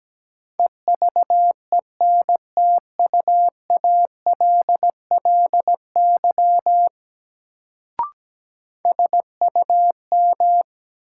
Morse code, heard as a keyed tone, about 17 wpm, 700 Hz. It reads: EVENTUALLY SUM
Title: Sentences from Top 1000 Words - Review Part 6 - New Word Drill - 17wpm